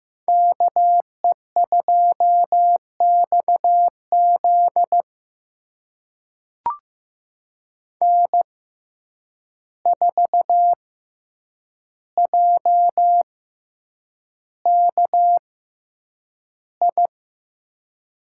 Morse code, heard as a keyed tone, about 15 wpm, spaced as for 8 wpm, 700 Hz. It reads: KE2XZ N4JKI
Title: USA Call Signs - 5 Characters - 6x Character Spacing - 15wpm